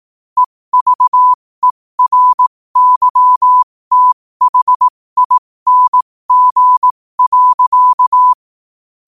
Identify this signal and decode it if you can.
Morse code, heard as a keyed tone, about 18 wpm, spaced as for 15 wpm, 1000 Hz. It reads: EVERYTHING.
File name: qotdfiles / 15wpm